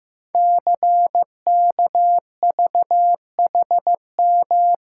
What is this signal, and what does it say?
Morse code, heard as a keyed tone, about 15 wpm, 700 Hz. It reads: CKVHM
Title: Code Groups - 5 Letters - 15wpm